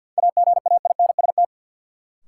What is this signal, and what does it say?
Morse code, heard as a keyed tone, about 53 wpm, 700 Hz. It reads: AGAINST
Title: Sets of 2 Words - New Word Drill - 50wpm